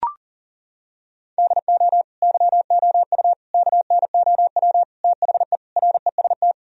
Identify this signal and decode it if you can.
Morse code, heard as a keyed tone, about 40 wpm, 700 Hz. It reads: DO YOU KNOW THE REST
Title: Sentences from Top 1000 Words - Review Part 2 - 40wpm